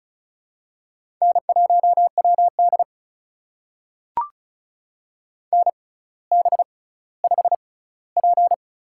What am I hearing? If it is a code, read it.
Morse code, heard as a keyed tone, about 36 wpm, spaced as for 18 wpm, 700 Hz. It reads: N1WD NB5P
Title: USA Call Signs - 4 Characters - 6x Character Spacing - 35wpm